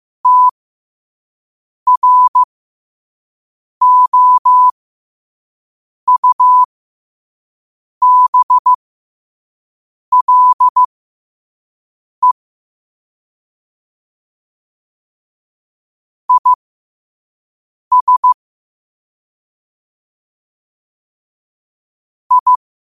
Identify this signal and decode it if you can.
Morse code, heard as a keyed tone, about 15 wpm, spaced as for 5 wpm, 1000 Hz. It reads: TROUBLE IS I
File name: qotdfiles / 5wpm